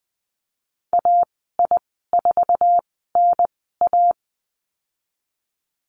Morse code, heard as a keyed tone, about 20 wpm, spaced as for 15 wpm, 700 Hz. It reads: AI4NA
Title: 179 Lesson 29g - Slash - Call Signs 20wpm